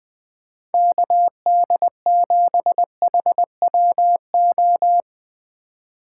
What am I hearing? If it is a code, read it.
Morse code, heard as a keyed tone, about 20 wpm, 700 Hz. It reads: KD7HWO
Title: Speed Racing - USA 2x3 Call Signs 17wpm